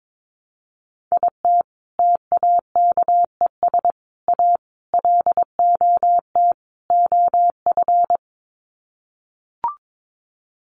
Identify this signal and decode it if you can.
Morse code, heard as a keyed tone, about 22 wpm, 700 Hz. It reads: IT TAKES A LOT OF
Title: Sets of 5 Words Encore 22wpm